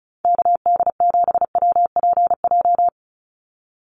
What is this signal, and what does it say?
Morse code, heard as a keyed tone, about 34 wpm, 700 Hz. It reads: KD7WPJ
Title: NA SOTA Activator Callsigns 35wpm